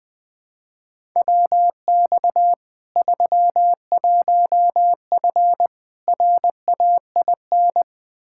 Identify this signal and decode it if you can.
Morse code, heard as a keyed tone, about 20 wpm, 700 Hz. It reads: WX 31F RAIN
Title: Practice QSO - Exchange 2 - Info - 20wpm